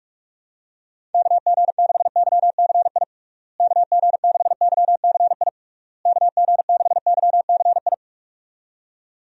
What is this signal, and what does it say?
Morse code, heard as a keyed tone, about 46 wpm, 700 Hz. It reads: KG6YCI KG6YCI KG6YCI
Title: USA Call Signs - 6 Characters - Repeated 3x - 45wpm